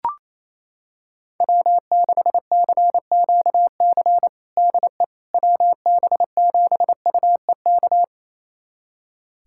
Morse code, heard as a keyed tone, about 28 wpm, 700 Hz. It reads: W6CQC DE WB7UEK
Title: USA Callsign DE Callsign 28wpm